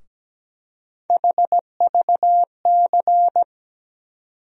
Morse code, heard as a keyed tone, about 17 wpm, 700 Hz. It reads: HVC
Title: USA Call Sign Postfixes - 3 Letters 17wpm